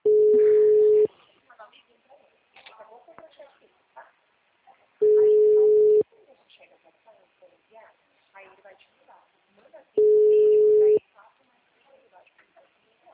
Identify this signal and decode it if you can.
Ringback tone (Europe, 425 Hz), with speech.